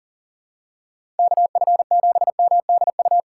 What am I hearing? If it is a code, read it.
Morse code, heard as a keyed tone, about 40 wpm, 700 Hz. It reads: KF7MDU
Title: USA Call Signs - 6 Characters - 3x Character Spacing - 40wpm